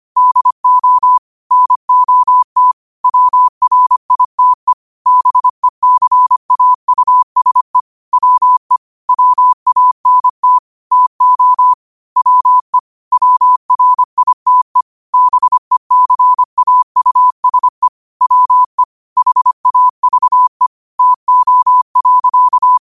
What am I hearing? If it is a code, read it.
Morse code, heard as a keyed tone, about 25 wpm, 1000 Hz. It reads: DO NOT WRITE BECAUSE WE WANT TO WE WRITE BECAUSE WE HAVE TO.
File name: qotdfiles / 25wpm